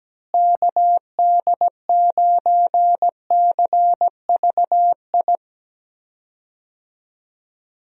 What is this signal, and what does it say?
Morse code, heard as a keyed tone, about 17 wpm, 700 Hz. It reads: KD9CVI
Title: USA Call Signs - 6 Characters - Repeated 3x - 17wpm